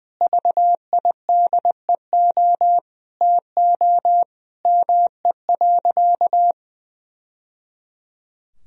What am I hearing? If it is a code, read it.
Morse code, heard as a keyed tone, about 20 wpm, 700 Hz. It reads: VIDEO TO ME.
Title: Sentences from Top 1000 Words - Review Part 4 - 20wpm